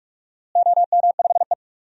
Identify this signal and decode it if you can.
Morse code, heard as a keyed tone, about 46 wpm, 700 Hz. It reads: OM5E
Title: DXCC-100 Call Signs - 4 Characters - Repeated 2x - 45wpm